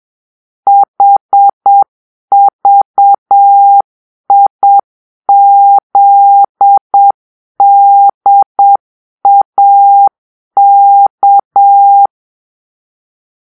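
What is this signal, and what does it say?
Morse code, heard as a keyed tone, about 7 wpm, 800 Hz. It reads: HVIZDAK
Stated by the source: Morse-code cipher made by a player team for the Kachna puzzle game.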